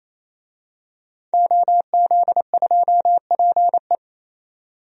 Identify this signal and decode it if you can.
Morse code, heard as a keyed tone, about 28 wpm, 700 Hz. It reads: OZ2PE
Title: DXCC-100 Call Signs - 5 Characters - 28wpm